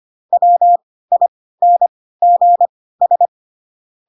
Morse code, heard as a keyed tone, about 25 wpm, spaced as for 16 wpm, 700 Hz. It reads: WINGS